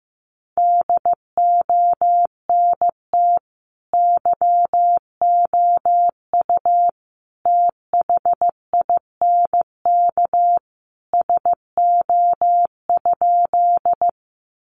Morse code, heard as a keyed tone, about 14 wpm, 700 Hz. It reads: DONT YOU THINK SO?
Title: Sentences from Top 300 Words 15wpm